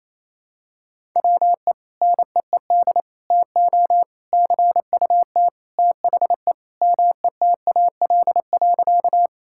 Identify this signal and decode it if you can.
Morse code, heard as a keyed tone, about 28 wpm, 700 Hz. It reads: WE NEED TO CUT THE METAL.
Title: Sentences from Top 1000 Words - Review Part 6 - 28wpm